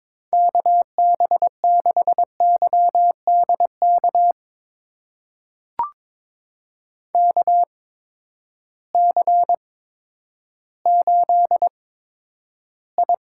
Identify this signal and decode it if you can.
Morse code, heard as a keyed tone, about 22 wpm, spaced as for 11 wpm, 700 Hz. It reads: KB6YDK KC8I
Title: USA Call Signs - 6 Characters - 8x Character Spacing - 22wpm